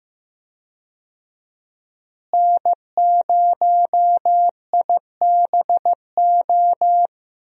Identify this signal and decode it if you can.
Morse code, heard as a keyed tone, about 15 wpm, 700 Hz. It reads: N0IBO
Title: USA Call Signs - 5 Characters - Repeated 3x - 15wpm